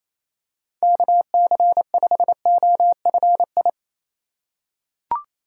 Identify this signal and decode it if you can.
Morse code, heard as a keyed tone, about 28 wpm, 700 Hz. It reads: KC5OFI
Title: USA Call Signs - 6 Characters - 3x Character Spacing - 28wpm